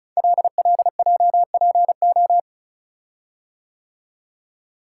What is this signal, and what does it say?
Morse code, heard as a keyed tone, about 36 wpm, 690 Hz. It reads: LLJPO
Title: Code Groups - 5 Letters - 35wpm